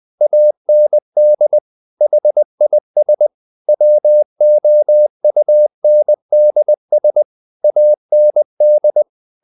Morse code, heard as a keyed tone, about 20 wpm, 600 Hz. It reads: AND HIS WOUNDS AND
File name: the old man and the sea part 2 20wpm